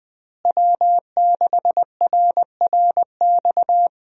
Morse code, heard as a keyed tone, about 21 wpm, 700 Hz. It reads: W6RRX
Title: USA Call Signs - 5 Characters - Repeated 2x - 20wpm